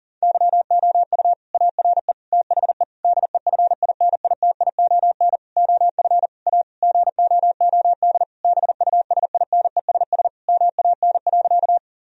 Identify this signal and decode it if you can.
Morse code, heard as a keyed tone, about 40 wpm, 700 Hz. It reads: YOU ARE THE DEFINITION OF A GOOD BUSINESS MAN.
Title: Sentences from Top 1000 Words - Review Part 4 - 40wpm